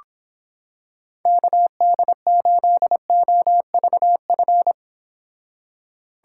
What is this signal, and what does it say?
Morse code, heard as a keyed tone, about 26 wpm, 700 Hz. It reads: KD8OVF